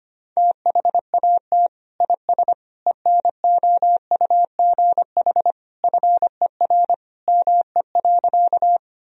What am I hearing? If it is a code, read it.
Morse code, heard as a keyed tone, about 25 wpm, 700 Hz. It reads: THAT IS ENOUGH FER ME.
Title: Sentences from Top 400 Words and Abbreviations 25wpm